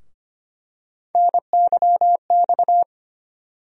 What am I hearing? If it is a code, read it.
Morse code, heard as a keyed tone, about 25 wpm, 700 Hz. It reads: NYX
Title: USA Call Sign Postfixes - 3 Letters 25wpm